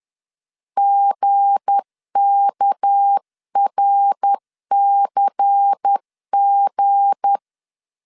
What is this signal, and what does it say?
Morse code, heard as a keyed tone, about 10 wpm, 800 Hz. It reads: GKRCG
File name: LEZIONI / 7 GF9Z / 7 G 10wpm 2501